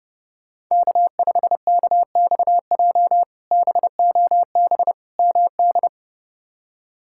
Morse code, heard as a keyed tone, about 31 wpm, 700 Hz. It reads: K5KXJ BOB MD
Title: K1USN SST QSO Practice - 30wpm